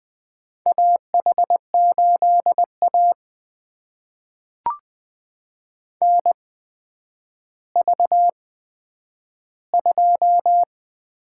Morse code, heard as a keyed tone, about 20 wpm, spaced as for 10 wpm, 700 Hz. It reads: AH8A NV2
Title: USA Call Signs - 4 Characters - 8x Character Spacing - 20wpm